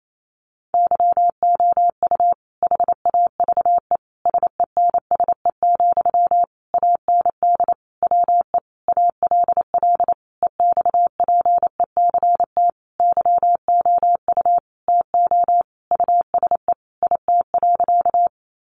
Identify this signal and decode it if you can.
Morse code, heard as a keyed tone, about 28 wpm, 700 Hz. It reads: YOU HAVE SENSE, AND WE ALL EXPECT YOU TO USE IT.
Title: Sentences from Top 1100 Words 28wpm